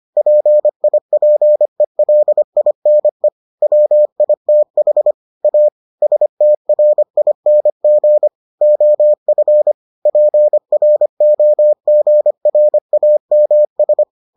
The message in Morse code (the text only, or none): PIPELINE WITH A STRING OF PROGRAMS